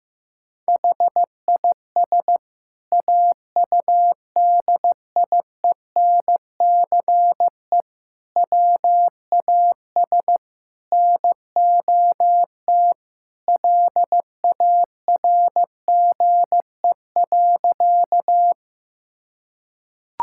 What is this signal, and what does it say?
Morse code, heard as a keyed tone, about 15 wpm, 700 Hz. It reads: HIS AUDIENCE WAS NOT LARGE.
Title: Sentences from Top 1000 Words - Review Part 5 - 15wpm